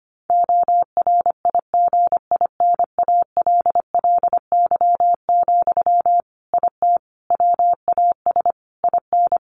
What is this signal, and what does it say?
Morse code, heard as a keyed tone, about 25 wpm, 700 Hz. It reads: ORIGINALLY, IT WAS IN
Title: Sentences from Top 1500 Words 25wpm